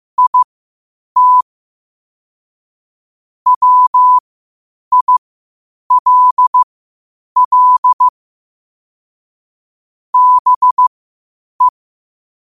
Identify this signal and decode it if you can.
Morse code, heard as a keyed tone, about 15 wpm, spaced as for 8 wpm, 1000 Hz. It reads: IT WILL BE
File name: qotdfiles / 8wpm